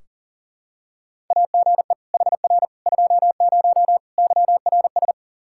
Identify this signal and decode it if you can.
Morse code, heard as a keyed tone, about 40 wpm, 700 Hz. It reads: AGE HR 20 YRS
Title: K1USN SST QSO Practice 40wpm